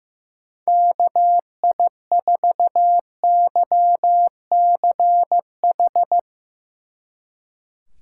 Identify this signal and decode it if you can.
Morse code, heard as a keyed tone, about 16 wpm, 700 Hz. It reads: KI4YCH
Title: Speed Racing - USA 2x3 Call Signs 15wpm